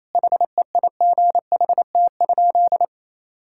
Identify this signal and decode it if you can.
Morse code, heard as a keyed tone, about 28 wpm, 700 Hz. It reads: HEIGHT?
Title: Sentences from Top 1600 Words 28wpm